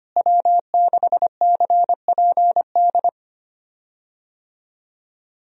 Morse code, heard as a keyed tone, about 25 wpm, 700 Hz. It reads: W6CPD
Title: USA Call Signs - 5 Characters - Repeated 3x - 25wpm